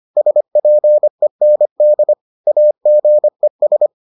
Morse code, heard as a keyed tone, about 25 wpm, 600 Hz. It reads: SPEND AGES